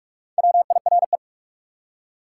Morse code, heard as a keyed tone, about 59 wpm, spaced as for 52 wpm, 700 Hz. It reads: WIRE
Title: Speed-Racing Top 180 QSO Elements 45wpm